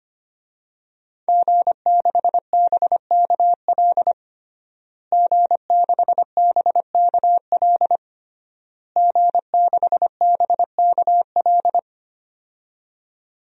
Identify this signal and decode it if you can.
Morse code, heard as a keyed tone, about 25 wpm, 700 Hz. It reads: G6BKL G6BKL G6BKL
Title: DXCC-100 Call Signs - 5 Characters - Repeated 3x - 25wpm